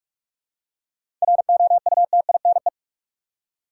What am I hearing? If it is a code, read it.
Morse code, heard as a keyed tone, about 45 wpm, 700 Hz. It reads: ROUTINE
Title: Seven Letter Words - 45wpm